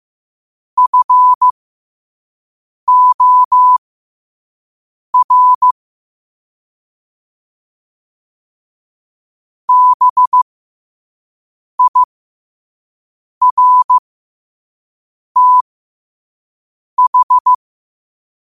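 Morse code, heard as a keyed tone, about 15 wpm, spaced as for 5 wpm, 1000 Hz. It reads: FOR BIRTH
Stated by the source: Morse code quote of the day sent at 5 wpm.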